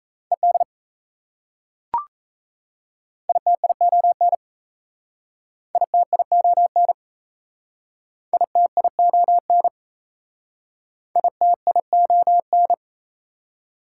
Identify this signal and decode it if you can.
Morse code, heard as a keyed tone, about 38 wpm, 700 Hz. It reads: ED ITION ITION ITION ITION